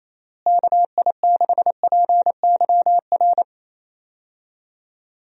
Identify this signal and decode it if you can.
Morse code, heard as a keyed tone, about 28 wpm, 700 Hz. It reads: KI6PYR
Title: USA Call Signs - 6 Characters - Repeated 2x - 28wpm